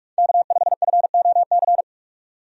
Morse code, heard as a keyed tone, about 44 wpm, 700 Hz. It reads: K5FOC
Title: USA Call Signs - 5 Characters - 45wpm